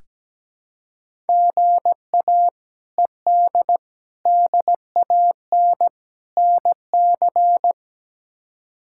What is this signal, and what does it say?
Morse code, heard as a keyed tone, about 17 wpm, 700 Hz. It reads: GA ED DAN NC
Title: K1USN SST QSO Practice - 17wpm